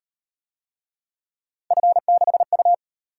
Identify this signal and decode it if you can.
Morse code, heard as a keyed tone, about 36 wpm, 700 Hz. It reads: F6U